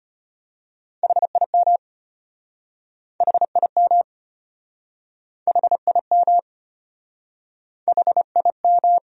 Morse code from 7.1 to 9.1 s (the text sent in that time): HIM